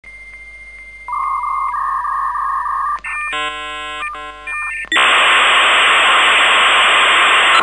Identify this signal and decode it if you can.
Dial-up modem handshake.